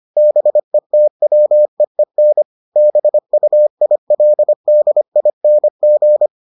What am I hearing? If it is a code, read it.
Morse code, heard as a keyed tone, about 25 wpm, 600 Hz. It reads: BETWEEN BUILDING